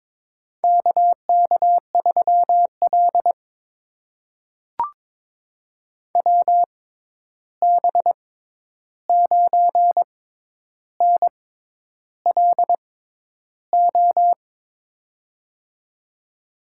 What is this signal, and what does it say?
Morse code, heard as a keyed tone, about 22 wpm, spaced as for 10 wpm, 700 Hz. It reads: KK3L WB9NLO
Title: USA Call Signs - 6x Character Spacing - 22wpm